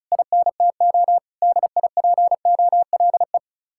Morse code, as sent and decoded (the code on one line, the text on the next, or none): .. -. - --- / -.. .. .--. --- .-.. .
INTO DIPOLE